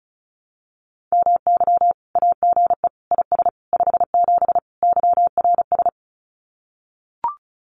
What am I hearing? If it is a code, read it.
Morse code, heard as a keyed tone, about 34 wpm, 700 Hz. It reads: MY AGE IS 57 YRS
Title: Common QSO Phrases Encore 35wpm